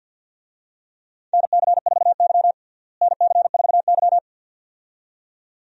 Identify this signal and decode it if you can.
Morse code, heard as a keyed tone, about 50 wpm, 700 Hz. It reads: NC4Y NC4Y
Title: USA Call Signs - 4 Characters - Repeated 2x - 50wpm